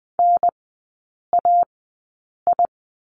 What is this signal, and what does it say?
Morse code, heard as a keyed tone, about 20 wpm, spaced as for 8 wpm, 700 Hz. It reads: NAI
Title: 041 Lesson 8e - Number 1 - Three Characters 20wpm